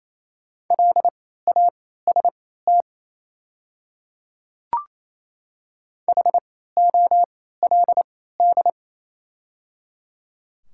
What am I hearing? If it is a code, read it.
Morse code, heard as a keyed tone, about 28 wpm, spaced as for 16 wpm, 700 Hz. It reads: LAST HOLD
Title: 3x Character Spacing - Top 500 Words - 28wpm